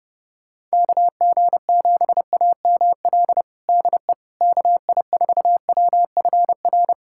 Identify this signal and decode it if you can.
Morse code, heard as a keyed tone, about 30 wpm, 700 Hz. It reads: KG7AML DE KI4WFR
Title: USA Callsign DE Callsign 30wpm